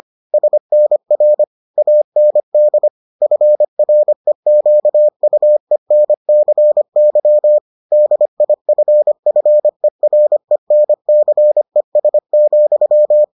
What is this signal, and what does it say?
Morse code, heard as a keyed tone, about 25 wpm, 600 Hz. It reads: SNR AND FREQUENCY DIFFERENCES,